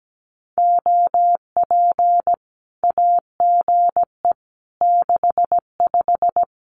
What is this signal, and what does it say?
Morse code, heard as a keyed tone, about 17 wpm, 700 Hz. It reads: OP AGE 65
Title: Common QSO Phrases 17wpm